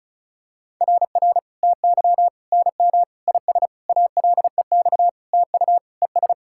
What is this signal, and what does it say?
Morse code, heard as a keyed tone, about 36 wpm, 700 Hz. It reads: RR TY NM IS ALEX TU ES
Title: SOTA QSO Practice 35wpm